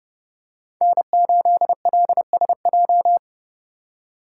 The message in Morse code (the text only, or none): N8LSJ